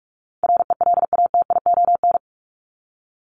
Morse code, heard as a keyed tone, about 44 wpm, 700 Hz. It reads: RELATION
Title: Top 1751-2000 Words 45wpm